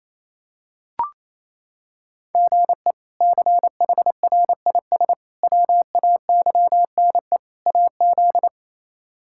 Morse code, heard as a keyed tone, about 28 wpm, 700 Hz. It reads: GE CHRIS WAYNE AZ
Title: K1USN SST QSO Practice - 28wpm